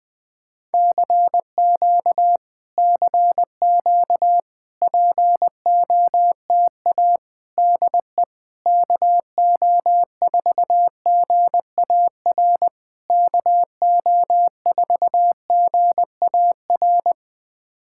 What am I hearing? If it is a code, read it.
Morse code, heard as a keyed tone, about 20 wpm, 700 Hz. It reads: CQ CQ POTA DE KO4GAR KO4GAR
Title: POTA QSO Practice - 20wpm